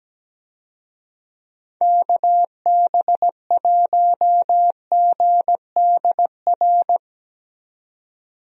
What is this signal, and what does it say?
Morse code, heard as a keyed tone, about 17 wpm, 700 Hz. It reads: KB1GDR
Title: USA Call Signs - 6 Characters - 17wpm